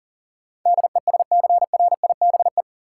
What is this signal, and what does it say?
Morse code, heard as a keyed tone, about 40 wpm, 700 Hz. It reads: DESCRIBE
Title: Sentences from Top 1000 Words - New Word Drill - 40wpm